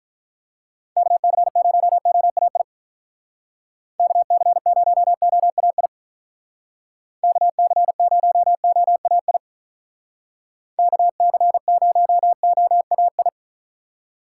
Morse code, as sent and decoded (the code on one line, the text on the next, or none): -.- -.-. ----- --- .- .. / -.- -.-. ----- --- .- .. / -.- -.-. ----- --- .- .. / -.- -.-. ----- --- .- ..
KC0OAI KC0OAI KC0OAI KC0OAI